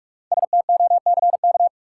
Morse code, heard as a keyed tone, about 48 wpm, 700 Hz. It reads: STOCK
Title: Sets of 5 Words - New Word Drill - 45wpm